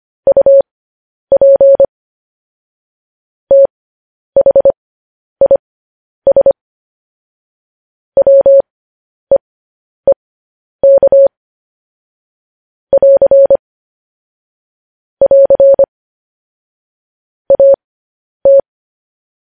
Morse code, heard as a keyed tone, about 25 wpm, spaced as for 10 wpm, 550 Hz. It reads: UP THIS WEEK <AR> <AR> AT
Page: news headlines in morse code at 10 wpm